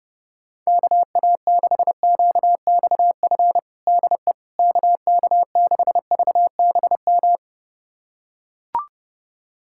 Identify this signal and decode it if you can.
Morse code, heard as a keyed tone, about 30 wpm, 700 Hz. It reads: KA6QXF DE KK6VBM
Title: USA Callsign DE Callsign - 6 Characters - 30wpm